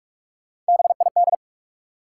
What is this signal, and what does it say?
Morse code, heard as a keyed tone, about 44 wpm, 700 Hz. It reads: BID
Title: Three Letter Words - 45wpm